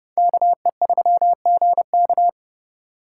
Morse code, heard as a keyed tone, about 38 wpm, 700 Hz. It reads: KE3GK